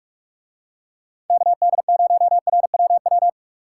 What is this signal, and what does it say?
Morse code, heard as a keyed tone, about 45 wpm, 700 Hz. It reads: KD0RWW